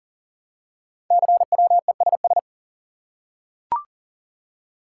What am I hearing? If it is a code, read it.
Morse code, heard as a keyed tone, about 38 wpm, 700 Hz. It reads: CWESS
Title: Code Groups - 5 Letters - 40wpm